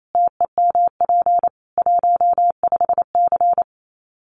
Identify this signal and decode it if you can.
Morse code, heard as a keyed tone, about 28 wpm, 700 Hz. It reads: TEMP 15C
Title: Common QSO Phrases 28wpm